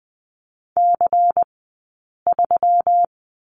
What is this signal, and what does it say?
Morse code, heard as a keyed tone, about 20 wpm, spaced as for 8 wpm, 700 Hz. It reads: C3
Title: 165 Lesson 27d - Number 7 - Two Characters 20wpm